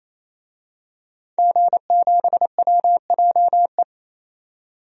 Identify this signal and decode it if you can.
Morse code, heard as a keyed tone, about 28 wpm, 700 Hz. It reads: G7WJE
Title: DXCC-100 Call Signs - 5 Characters - 28wpm